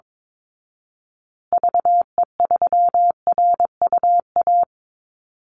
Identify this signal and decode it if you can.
Morse code, heard as a keyed tone, about 22 wpm, 700 Hz. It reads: VE3RUA
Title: Call Signs Repeated 2x - 22wpm